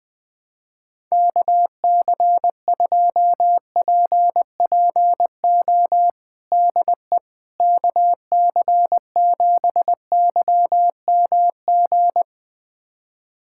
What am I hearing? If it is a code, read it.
Morse code, heard as a keyed tone, about 20 wpm, 700 Hz. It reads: KC2PPO DE KC7YMG
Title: USA Callsign DE Callsign - 6 Characters - 20wpm